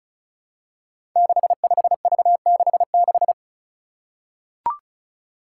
Morse code, heard as a keyed tone, about 36 wpm, 700 Hz. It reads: 65V66
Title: Five Challenging Letter-Number Combinations 35wpm